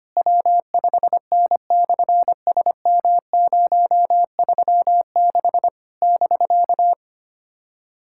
Morse code, heard as a keyed tone, about 24 wpm, 700 Hz. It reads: W5N/SM036 <BK>
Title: SOTA QSO Practice 25wpm